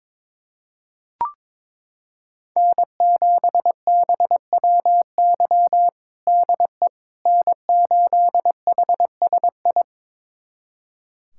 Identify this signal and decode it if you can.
Morse code, heard as a keyed tone, about 21 wpm, 700 Hz. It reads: N7BWY DE N8HSI
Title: USA Callsign DE Callsign - 5 Characters - 22wpm